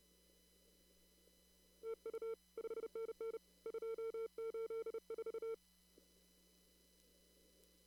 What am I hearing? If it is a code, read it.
Morse code, heard as a keyed tone, about 29 wpm, 450 Hz. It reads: TU 5NN 284
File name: CW / kt5j wpx cw-op-k5pi